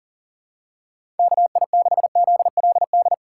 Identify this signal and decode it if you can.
Morse code, heard as a keyed tone, about 40 wpm, 700 Hz. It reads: KI6ZLD
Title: USA Call Signs - 6 Characters - 2x Character Spacing - 40wpm